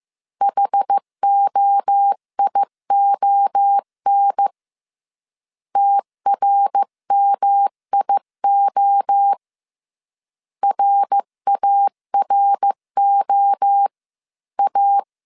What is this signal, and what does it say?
Morse code, heard as a keyed tone, about 15 wpm, 800 Hz. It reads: HOION TRMIO RAROA
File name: CW / DET / 1 3 1500C S A 14wpm 2447